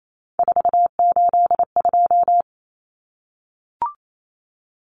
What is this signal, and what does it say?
Morse code, heard as a keyed tone, about 28 wpm, 700 Hz. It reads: 482
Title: Common Cut Numbers - 3-Digit 28wpm